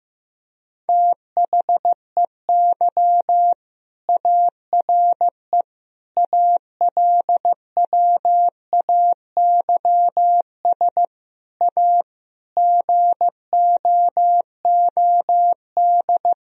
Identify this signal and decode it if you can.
Morse code, heard as a keyed tone, about 15 wpm, 700 Hz. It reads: THEY ARE ALWAYS A GOOD
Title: Sentences from Top 1000 Words - Review Part 4 - 15wpm